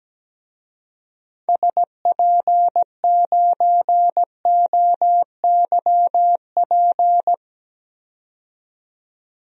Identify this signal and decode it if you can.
Morse code, heard as a keyed tone, about 17 wpm, 700 Hz. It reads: SP9OYP
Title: DXCC-100 Call Signs - 6 Characters - 17wpm